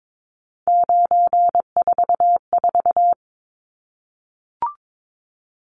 Morse code, heard as a keyed tone, about 22 wpm, 700 Hz. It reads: 944
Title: Numbers - 3-Digit 22wpm